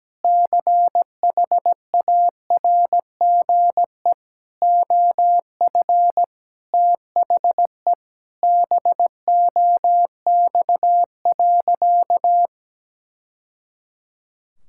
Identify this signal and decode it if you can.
Morse code, heard as a keyed tone, about 17 wpm, 700 Hz. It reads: CHARGE OF THE BOX.
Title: Sentences from Top 1000 Words - Review Part 6 - 17wpm